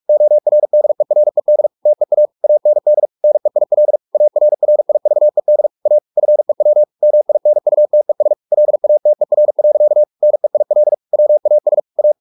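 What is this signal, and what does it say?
Morse code, heard as a keyed tone, about 44 wpm, 600 Hz. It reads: ORDERED TEA AND NEIL ARRIVED A FEW MINUTES LATER. NEIL WAS A
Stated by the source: Morse practice at 45 wpm, the novel Never by Ken Follett in English, from CW med Gnister.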